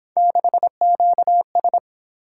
Morse code, heard as a keyed tone, about 26 wpm, 700 Hz. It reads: 6QS
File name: Speed-Racing - Three Letter-Number Combinations - 22wpm